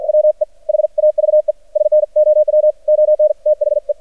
Morse code, heard as a keyed tone, about 54 wpm, 600 Hz. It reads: WE HAVE SNOW ON THE
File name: archived / CRSnet